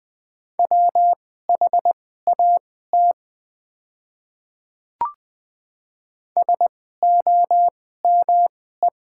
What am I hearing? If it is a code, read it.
Morse code, heard as a keyed tone, about 20 wpm, spaced as for 14 wpm, 700 Hz. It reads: WHAT SOME
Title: 2x Character Spacing - Top 200 Words - 20wpm